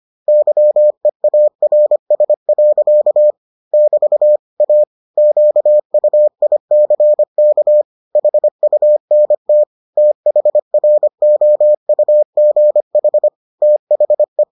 Morse code, heard as a keyed tone, about 25 wpm, 600 Hz. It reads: YEARS. <BT> A QUICK HUNT THROUGH THE